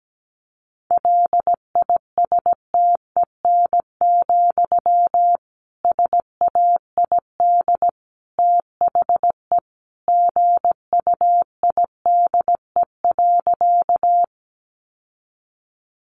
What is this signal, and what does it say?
Morse code, heard as a keyed tone, about 17 wpm, 700 Hz. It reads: LISTEN, SAID THE GUIDE.
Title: Sentences from Top 600 Words 17wpm